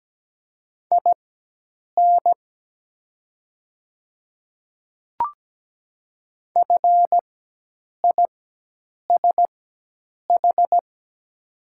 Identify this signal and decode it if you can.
Morse code, heard as a keyed tone, about 17 wpm, spaced as for 8 wpm, 700 Hz. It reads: IN FISH
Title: 4x Character Spacing - Top 300 Words - 17wpm